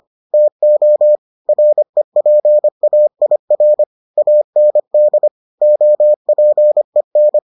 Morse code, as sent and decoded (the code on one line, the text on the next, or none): - --- / .-. .--. .- .. .-. / .- -. -.. / --- .--. -.
TO REPAIR AND OPEN